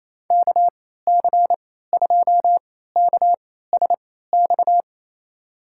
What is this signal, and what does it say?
Morse code, heard as a keyed tone, about 28 wpm, spaced as for 16 wpm, 700 Hz. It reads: KC2KSX
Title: USA Call Signs - 3x Character Spacing - 28wpm